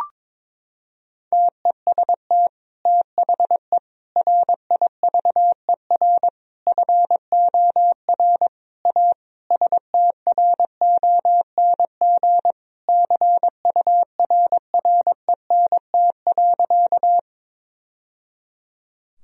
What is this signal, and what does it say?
Morse code, heard as a keyed tone, about 22 wpm, 700 Hz. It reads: TEST THE RIVER FOR A STRONG CURRENT.
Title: Sentences from Top 1000 Words - Review Part 4 - 22wpm